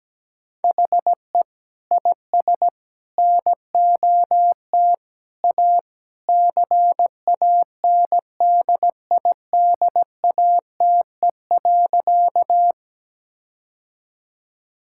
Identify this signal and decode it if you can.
Morse code, heard as a keyed tone, about 17 wpm, 700 Hz. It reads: HE IS NOT A CANDIDATE.
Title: Sentences from Top 1600 Words 17wpm